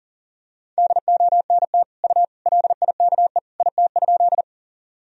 Morse code, heard as a keyed tone, about 40 wpm, 700 Hz. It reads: DONT U LIKE IT?
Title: Sentences from Top 300 Words and Abbreviations 40wpm